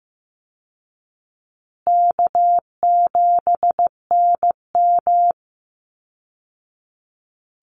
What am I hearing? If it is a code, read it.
Morse code, heard as a keyed tone, about 15 wpm, 700 Hz. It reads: K7NM
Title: Easy Call Signs Repeated 3x - 15wpm